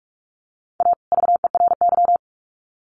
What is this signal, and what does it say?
Morse code, heard as a keyed tone, about 44 wpm, 700 Hz. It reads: A VERY
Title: Sets of 2 Words 45wpm